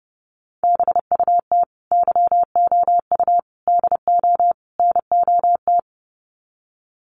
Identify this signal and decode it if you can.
Morse code, heard as a keyed tone, about 30 wpm, 700 Hz. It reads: BUT YOU DO NOT
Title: Sets of 4 Words 30wpm